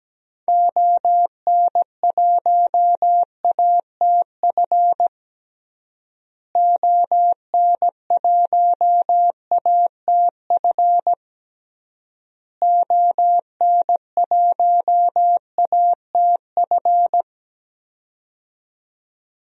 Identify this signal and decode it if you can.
Morse code, heard as a keyed tone, about 17 wpm, 700 Hz. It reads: ON1ATF ON1ATF ON1ATF
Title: DXCC-100 Call Signs - 6 Characters - Repeated 3x - 17wpm